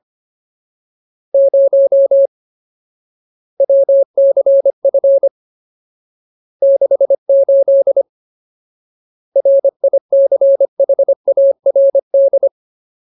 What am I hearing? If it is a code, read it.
Morse code, heard as a keyed tone, about 25 wpm, 550 Hz. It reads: 0 WCF 68 RICHARD
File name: Ultimate Word List / 25WPM Ultimate